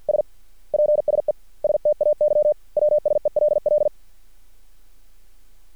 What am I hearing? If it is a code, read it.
Morse code, heard as a keyed tone, about 50 wpm, 600 Hz. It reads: S PSE STAY WSELL
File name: archived / CRSnet